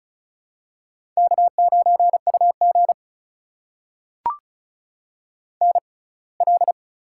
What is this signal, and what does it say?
Morse code, heard as a keyed tone, about 36 wpm, spaced as for 24 wpm, 700 Hz. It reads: K9UG NL
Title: USA Call Signs - 4 Characters - 6x Character Spacing - 35wpm